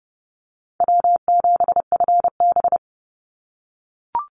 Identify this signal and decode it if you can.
Morse code, heard as a keyed tone, about 30 wpm, 700 Hz. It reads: W7FB
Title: Call Signs - 30wpm